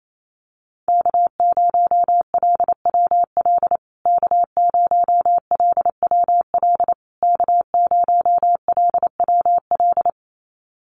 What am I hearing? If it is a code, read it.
Morse code, heard as a keyed tone, about 28 wpm, 700 Hz. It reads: K0LWL K0LWL K0LWL
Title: USA Call Signs Repeated 3x 28wpm